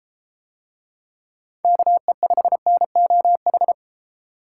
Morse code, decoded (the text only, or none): KE5NOH